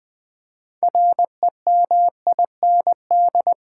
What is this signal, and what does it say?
Morse code, heard as a keyed tone, about 20 wpm, 700 Hz. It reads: REMIND
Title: 6 Letter Words - First 3 Letters Given - 20wpm